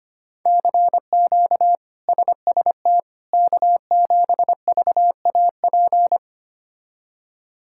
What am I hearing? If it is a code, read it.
Morse code, heard as a keyed tone, about 25 wpm, 700 Hz. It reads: CQ SST K7VAP
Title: K1USN SST QSO Practice - 25wpm